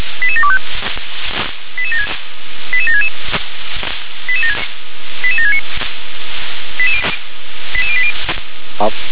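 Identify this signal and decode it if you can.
Five-tone selective call (ZVEI), heard as quick runs of five tones, tones of about 70 ms. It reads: 88725 88766 88769 88080
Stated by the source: ZVEI five-tone call 88769 received by SDR.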